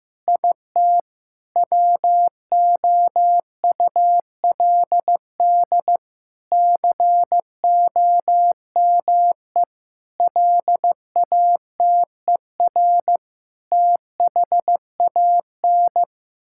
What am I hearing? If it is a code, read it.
Morse code, heard as a keyed tone, about 15 wpm, 700 Hz. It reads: IT WOULD COME LATER THAN
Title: Sentences from Top 1600 Words 15wpm